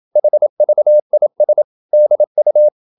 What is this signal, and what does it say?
Morse code, heard as a keyed tone, about 27 wpm, 600 Hz. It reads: HVIS DU
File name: Ferie QRQ 27wpm